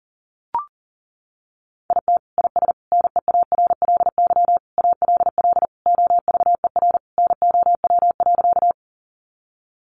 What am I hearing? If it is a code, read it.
Morse code, heard as a keyed tone, about 40 wpm, 700 Hz. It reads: IT IS NEARLY ALL OVER NOW.
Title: Sentences from Top 1300 Words 40wpm